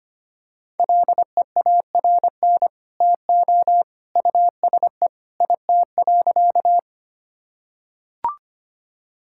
Morse code, heard as a keyed tone, about 25 wpm, 700 Hz. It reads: LEARN TO USE IT.
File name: practice / 25wpm / Sentences with 2-Word Phrases - 25wpm